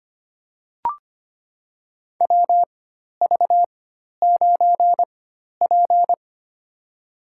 Morse code, heard as a keyed tone, about 25 wpm, spaced as for 12 wpm, 700 Hz. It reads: WV9P
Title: USA Call Signs - 4 Characters - 4x Character Spacing - 25wpm